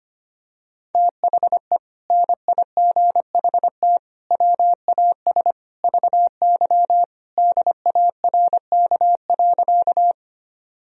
Morse code, heard as a keyed tone, about 25 wpm, 700 Hz. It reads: THE NIGHT WAS VY DARK.
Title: Sentences from Top 500 Words and Abbreviations 25wpm